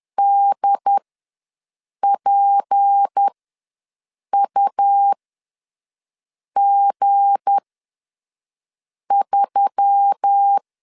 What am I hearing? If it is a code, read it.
Morse code, heard as a keyed tone, about 10 wpm, spaced as for 5 wpm, 800 Hz. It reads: DPUG3
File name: LEZIONI / 8 YJ45 / 8 YJ ws 10wpm 4043